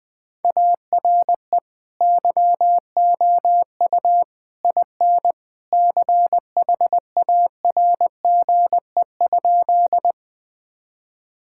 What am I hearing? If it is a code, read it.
Morse code, heard as a keyed tone, about 20 wpm, 700 Hz. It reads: ARE YOU IN CHARGE?
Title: Sentences from Top 1000 Words - Review Part 6 - 20wpm